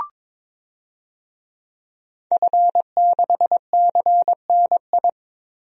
Morse code, heard as a keyed tone, about 22 wpm, 700 Hz. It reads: F6CNI
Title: DXCC-100 Call Signs - 5 Characters - Repeated 2x - 22wpm